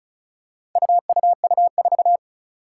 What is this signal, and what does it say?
Morse code, heard as a keyed tone, about 36 wpm, 700 Hz. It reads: UUU4
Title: Four Challenging Letter-Number Combinations 35wpm